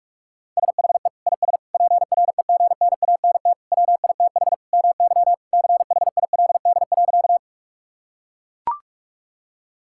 Morse code, heard as a keyed tone, about 42 wpm, 700 Hz. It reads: SHE IS PREGNANT WITH MY CHILD.
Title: Sentences from Top 1000 Words - Review Part 6 - 45wpm